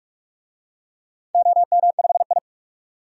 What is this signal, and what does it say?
Morse code, heard as a keyed tone, about 46 wpm, 700 Hz. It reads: OM5I
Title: DXCC-100 Call Signs - 4 Characters - 45wpm